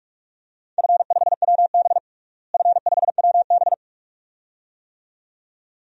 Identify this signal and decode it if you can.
Morse code, heard as a keyed tone, about 44 wpm, 700 Hz. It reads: F5WB F5WB
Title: DXCC-100 Call Signs - 4 Characters - Repeated 2x - 45wpm